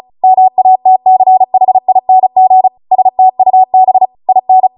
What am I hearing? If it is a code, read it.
Morse code, heard as a keyed tone, about 36 wpm, 750 Hz. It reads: MATCHING STUB IN